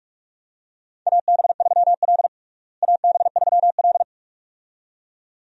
Morse code, heard as a keyed tone, about 44 wpm, 700 Hz. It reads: AB3L AB3L
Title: USA Call Signs - 4 Characters - Repeated 2x - 45wpm